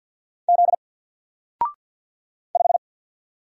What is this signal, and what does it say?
Morse code, heard as a keyed tone, about 50 wpm, 700 Hz. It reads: Z 5
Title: Single Letter-Number Alt 50wpm